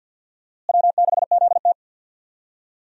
Morse code, heard as a keyed tone, about 51 wpm, 700 Hz. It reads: W6ZT
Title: USA Call Signs - 4 Characters - Repeated 2x - 50wpm